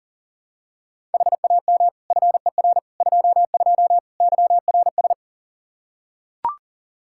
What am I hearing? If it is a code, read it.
Morse code, heard as a keyed tone, about 40 wpm, 700 Hz. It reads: HAM FER 22 YRS
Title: K1USN SST QSO Practice 40wpm